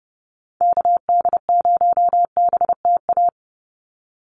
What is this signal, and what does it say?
Morse code, heard as a keyed tone, about 30 wpm, 700 Hz. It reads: KD0BTA
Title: USA Call Signs 30wpm